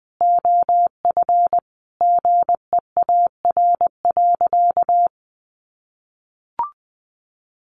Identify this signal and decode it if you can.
Morse code, heard as a keyed tone, about 20 wpm, 700 Hz. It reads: OF GEAR.
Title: Sentences from Top 1400 Words 20wpm